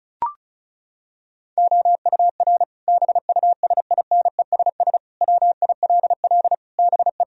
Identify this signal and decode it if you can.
Morse code, heard as a keyed tone, about 34 wpm, 700 Hz. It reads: OUR BUSINESS WILL BE
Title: Sentences from Top 1000 Words - Review Part 4 - 35wpm